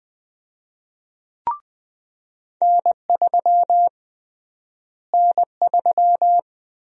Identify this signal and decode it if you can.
Morse code, heard as a keyed tone, about 20 wpm, 700 Hz. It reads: N3 N3
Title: USA Call Sign Prefixes - Repeated 2x - 20wpm